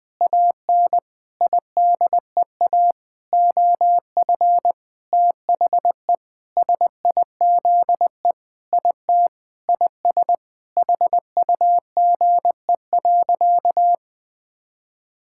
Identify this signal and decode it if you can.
Morse code, heard as a keyed tone, about 20 wpm, 700 Hz. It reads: AN IDEA OF THE SIZE IT IS HUGE.
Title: Sentences with 6-Word Phrases - 20wpm